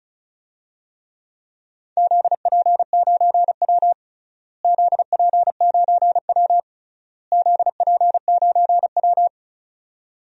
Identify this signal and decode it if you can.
Morse code, heard as a keyed tone, about 36 wpm, 700 Hz. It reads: ZP9W ZP9W ZP9W
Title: DXCC-100 Call Signs - 4 Characters - Repeated 3x - 35wpm